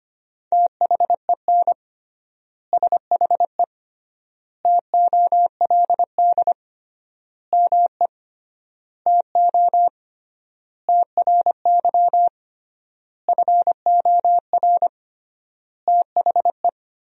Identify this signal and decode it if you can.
Morse code, heard as a keyed tone, about 25 wpm, 700 Hz. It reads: THEN SHE TOLD ME TO TRY FOR THE
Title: 3x Word-Spacing - Sentences from Top 400 Words - 25wpm